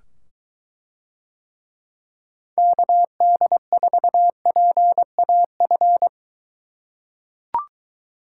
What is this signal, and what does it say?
Morse code, heard as a keyed tone, about 22 wpm, 700 Hz. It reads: KD4PAF